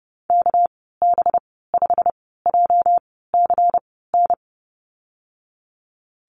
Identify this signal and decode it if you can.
Morse code, heard as a keyed tone, about 30 wpm, spaced as for 17 wpm, 700 Hz. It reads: KB5JCN
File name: Lesson 37g - Letter x - Call Signs 30wpm